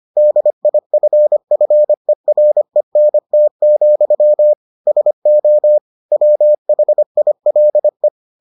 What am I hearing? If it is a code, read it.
Morse code, heard as a keyed tone, about 25 wpm, 600 Hz. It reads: DIFFERENT, SO WHILE